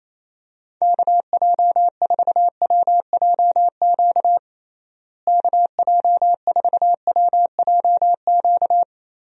Morse code, heard as a keyed tone, about 28 wpm, 700 Hz. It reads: KJ4WJQ KJ4WJQ
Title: USA Call Signs - 6 Characters - Repeated 2x - 28wpm